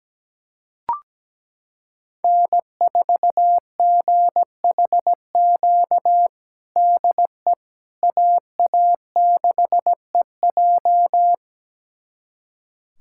Morse code, heard as a keyed tone, about 17 wpm, 700 Hz. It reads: N4GHQ DE AA6EJ
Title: USA Callsign DE Callsign - 5 Characters - 17wpm